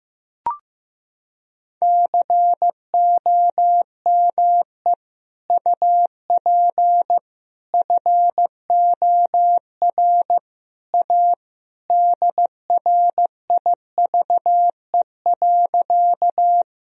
Morse code, heard as a keyed tone, about 15 wpm, 700 Hz. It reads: COME UP FOR A DRIVE.
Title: Sentences from Top 1000 Words - Review Part 2 - 15wpm